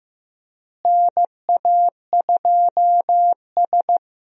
Morse code, heard as a keyed tone, about 15 wpm, 700 Hz. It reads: NA2S
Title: USA Call Signs - 4 Characters - Repeated 3x - 15wpm